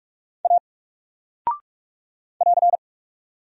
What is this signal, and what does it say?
Morse code, heard as a keyed tone, about 44 wpm, 700 Hz. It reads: A <AR>
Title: Single Character (Letter:Number:Punctuation:Prosign) 45wpm